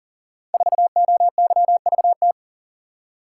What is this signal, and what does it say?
Morse code, heard as a keyed tone, about 38 wpm, 700 Hz. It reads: 4OYVT
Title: Code Groups - 5 Letter:Numbers - 40wpm